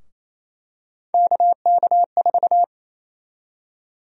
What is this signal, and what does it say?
Morse code, heard as a keyed tone, about 27 wpm, 700 Hz. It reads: KK4
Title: USA Call Sign Prefixes - Repeated 2x - 28wpm